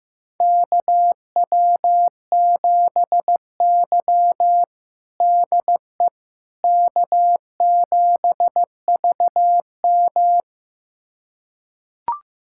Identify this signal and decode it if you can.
Morse code, heard as a keyed tone, about 15 wpm, 700 Hz. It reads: KW7Y DE K7VM
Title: USA Callsign DE Callsign - 4 Characters - 15wpm